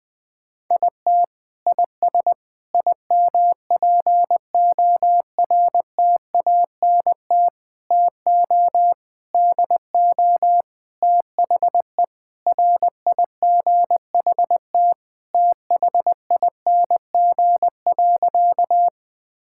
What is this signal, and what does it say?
Morse code, heard as a keyed tone, about 20 wpm, 700 Hz. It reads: IT IS IMPORTANT TO DO THE RIGHT THING.
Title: Sentences with 5-Word Phrases - Spoken Beginning - 20wpm